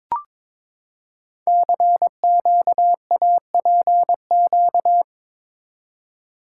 Morse code, heard as a keyed tone, about 22 wpm, 700 Hz. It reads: CQAPQ
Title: Code Groups - 5 Letters - 22wpm